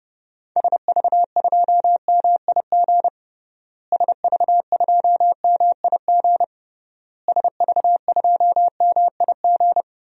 Morse code, heard as a keyed tone, about 30 wpm, 700 Hz. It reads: SV2MIG SV2MIG SV2MIG
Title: DXCC-100 Call Signs - 6 Characters - Repeated 3x - 30wpm